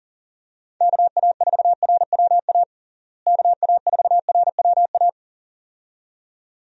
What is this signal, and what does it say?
Morse code, heard as a keyed tone, about 40 wpm, 700 Hz. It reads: KA4RWA KA4RWA
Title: USA Call Signs - 6 Characters - Repeated 2x - 40wpm